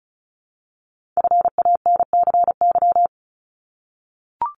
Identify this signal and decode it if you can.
Morse code, heard as a keyed tone, about 36 wpm, 700 Hz. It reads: FANCY
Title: Multi-letter Phonemes 35wpm